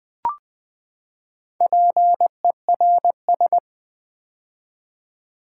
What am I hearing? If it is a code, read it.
Morse code, heard as a keyed tone, about 21 wpm, 700 Hz. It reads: PERS
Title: Build A Word - 4 Letters At A Time - 11-Letter Words - 20wpm